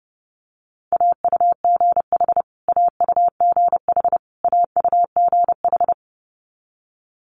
Morse code, heard as a keyed tone, about 30 wpm, 700 Hz. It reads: AUGH AUGH AUGH
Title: Multi-letter Phonemes Repeated 3x 30wpm